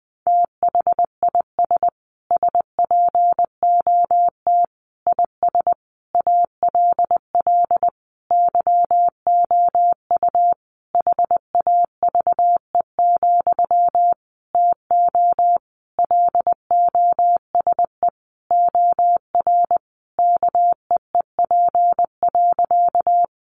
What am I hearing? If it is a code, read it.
Morse code, heard as a keyed tone, about 20 wpm, 700 Hz. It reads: THIS SPOT IS ALL YOU HAVE, TO LOSE OR KEEP.
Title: Sentences from Top 1100 Words 20wpm